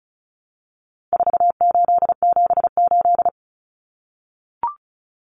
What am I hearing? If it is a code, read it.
Morse code, heard as a keyed tone, about 34 wpm, 700 Hz. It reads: 4878
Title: Common Cut Numbers - 4-Digit 35wpm